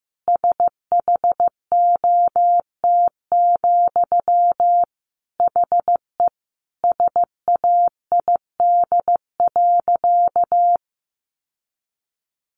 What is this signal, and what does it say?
Morse code, heard as a keyed tone, about 15 wpm, 700 Hz. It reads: SHOT, HE SAID.
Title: Sentences from Top 1100 Words 15wpm